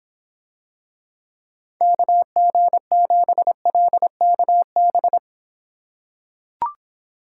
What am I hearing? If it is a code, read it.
Morse code, heard as a keyed tone, about 26 wpm, 700 Hz. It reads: KG7LKB